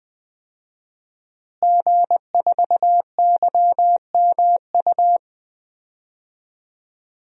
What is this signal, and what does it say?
Morse code, heard as a keyed tone, about 20 wpm, 700 Hz. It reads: G4YMU
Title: DXCC-100 Call Signs - 5 Characters - Repeated 2x - 20wpm